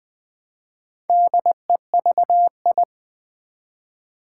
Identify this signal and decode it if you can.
Morse code, heard as a keyed tone, about 20 wpm, 700 Hz. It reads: DEVI
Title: Build A Word - 1 Letter At A Time - 5-Letter Words - 20wpm